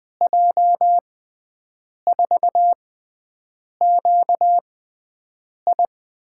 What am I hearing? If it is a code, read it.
Morse code, heard as a keyed tone, about 20 wpm, spaced as for 7 wpm, 700 Hz. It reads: J4QI
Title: USA Call Signs - 6 Characters - 6x Character Spacing - 20wpm